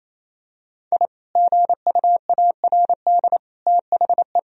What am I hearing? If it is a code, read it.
Morse code, heard as a keyed tone, about 28 wpm, 700 Hz. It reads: I GUARD THE
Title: Sentences from Top 1000 Words - Review Part 4 - 28wpm